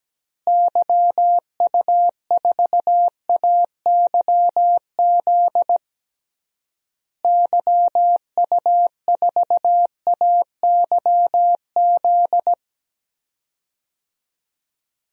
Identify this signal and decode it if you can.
Morse code, heard as a keyed tone, about 17 wpm, 700 Hz. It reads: YU4AYZ YU4AYZ
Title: DXCC-100 Call Signs - 6 Characters - Repeated 3x - 17wpm